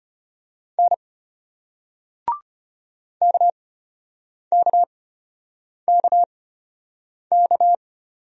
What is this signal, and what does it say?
Morse code, heard as a keyed tone, about 33 wpm, spaced as for 8 wpm, 700 Hz. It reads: N KKKK